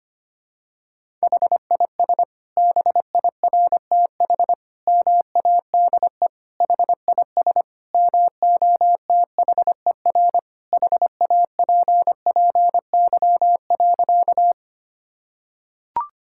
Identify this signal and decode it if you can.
Morse code, heard as a keyed tone, about 25 wpm, 700 Hz. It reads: HIS BIRTH MADE HIS MOTHER HAPPY.
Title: Sentences from Top 1000 Words - Review Part 5 - 25wpm